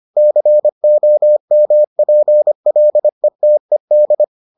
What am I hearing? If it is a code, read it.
Morse code, heard as a keyed tone, about 25 wpm, 600 Hz. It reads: COMPLETED